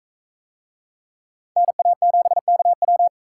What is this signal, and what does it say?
Morse code, heard as a keyed tone, about 43 wpm, 700 Hz. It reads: NA7KW